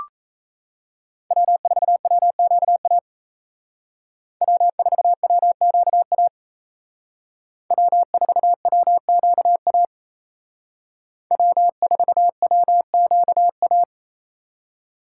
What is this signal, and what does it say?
Morse code, heard as a keyed tone, about 35 wpm, 700 Hz. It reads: W4WQA W4WQA W4WQA W4WQA